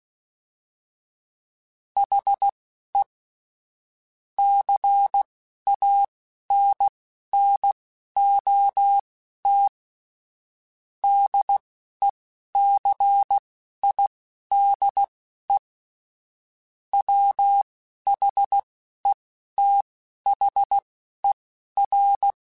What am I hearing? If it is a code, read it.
Morse code, heard as a keyed tone, about 16 wpm, spaced as for 11 wpm, 800 Hz. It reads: HE CANNOT DECIDE WHETHER
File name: Chuckles / Chuckles-mp3s / Glass Half Full @10